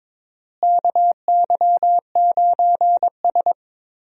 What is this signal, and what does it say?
Morse code, heard as a keyed tone, about 22 wpm, 700 Hz. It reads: KY9S
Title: USA Call Signs - 4 Characters - Repeated 2x - 22wpm